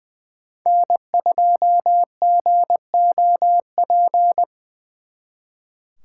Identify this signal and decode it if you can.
Morse code, heard as a keyed tone, about 24 wpm, 700 Hz. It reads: N2GOP
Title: Speed Racing - USA Call Signs 20wpm